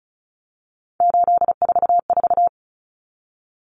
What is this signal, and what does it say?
Morse code, heard as a keyed tone, about 34 wpm, 700 Hz. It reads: 844
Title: Common Cut Numbers - 3-Digit 35wpm